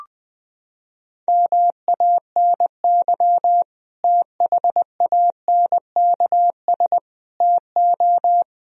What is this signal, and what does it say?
Morse code, heard as a keyed tone, about 20 wpm, 700 Hz. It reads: MANY THANKS TO
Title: Sentences from Top 1000 Words - Review Part 3 - 20wpm